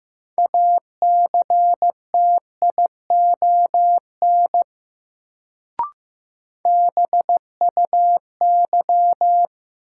Morse code, heard as a keyed tone, about 15 wpm, 700 Hz. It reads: ACTION BUY
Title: Sentences from Top 700 Words - New Word Drill - 15wpm